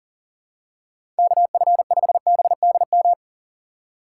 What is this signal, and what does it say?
Morse code, heard as a keyed tone, about 40 wpm, 700 Hz. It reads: KF5BDM
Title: USA Call Signs - 6 Characters - 2x Character Spacing - 40wpm